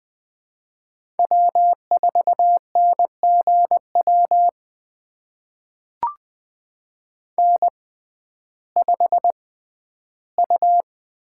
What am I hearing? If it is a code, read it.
Morse code, heard as a keyed tone, about 20 wpm, spaced as for 12 wpm, 700 Hz. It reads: W4NGW N5U
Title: USA Call Signs - 5 Characters - 6x Character Spacing - 20wpm